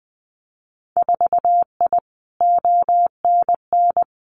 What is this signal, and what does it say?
Morse code, heard as a keyed tone, about 20 wpm, 700 Hz. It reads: 4I ONN
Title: Winter Field Day Exchanges Encore 20wpm